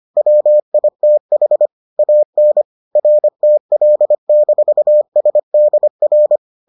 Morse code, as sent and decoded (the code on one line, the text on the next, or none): .-- .. - .... / .- -. / .-. - .-.. -....- ... -.. .-.
WITH AN RTL-SDR